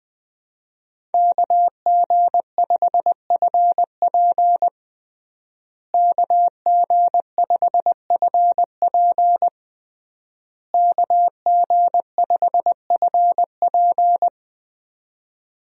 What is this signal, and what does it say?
Morse code, heard as a keyed tone, about 20 wpm, 700 Hz. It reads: KG5FP KG5FP KG5FP
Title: USA Call Signs - 5 Characters - Repeated 3x - 20wpm